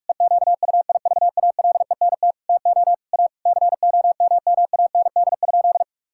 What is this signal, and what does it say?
Morse code, heard as a keyed tone, about 46 wpm, 700 Hz. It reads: EQUIVALENT TO A COMMAND?
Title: Sentences from Top 1600 Words 45wpm